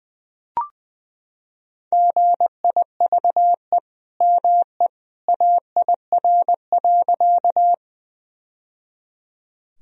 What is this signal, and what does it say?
Morse code, heard as a keyed tone, about 21 wpm, 700 Hz. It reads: GIVE ME AIR.
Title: Sentences from Top 1000 Words - Review Part 1 - 20wpm